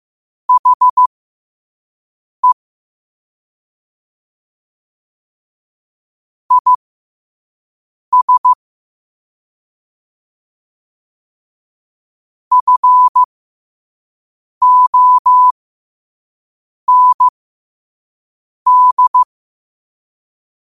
Morse code, heard as a keyed tone, about 15 wpm, spaced as for 5 wpm, 1000 Hz. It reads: HE IS FOND
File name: qotdfiles / 5wpm